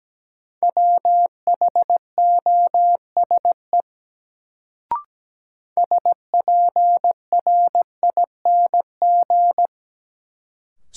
Morse code, heard as a keyed tone, about 17 wpm, 700 Hz. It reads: WHOSE SPRING
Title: Sentences from Top 1200 Words - New Word Drill - 17wpm